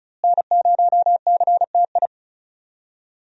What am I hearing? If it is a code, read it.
Morse code, heard as a keyed tone, about 36 wpm, 700 Hz. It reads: N0CTI
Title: USA Call Signs - 5 Characters - Repeated 3x - 35wpm